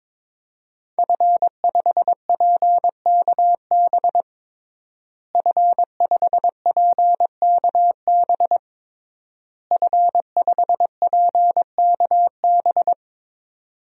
Morse code, heard as a keyed tone, about 22 wpm, 700 Hz. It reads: F5PKB F5PKB F5PKB
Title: DXCC-100 Call Signs - 5 Characters - Repeated 3x - 22wpm